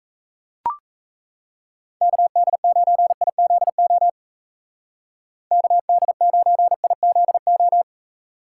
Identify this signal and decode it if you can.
Morse code, heard as a keyed tone, about 35 wpm, 700 Hz. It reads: KD9IZO KD9IZO
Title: Speed Racing - USA 2x3 Call Signs 28wpm